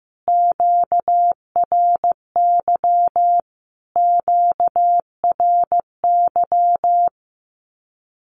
Morse code, heard as a keyed tone, about 15 wpm, 700 Hz. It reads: QRY QRY
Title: Q-Codes Repeated 3x 15wpm